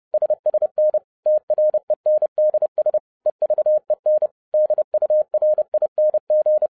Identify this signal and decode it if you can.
Morse code, heard as a keyed tone, about 30 wpm, 610 Hz. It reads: SSN TRENDS EVEN DURING
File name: 2012-01-mp3 / n11-I DIDNT MISS THE SOLAR CYCLE 24-30wpm